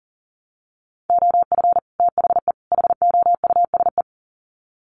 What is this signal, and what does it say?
Morse code, heard as a keyed tone, about 38 wpm, 700 Hz. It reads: OF THE HOUSE
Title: Sets of 3 Words 40wpm